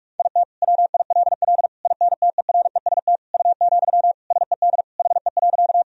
Morse code, heard as a keyed tone, about 46 wpm, 700 Hz. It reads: IT WILL INTEREST U, SED HE.
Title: Sentences from Top 400 Words and Abbreviations 45wpm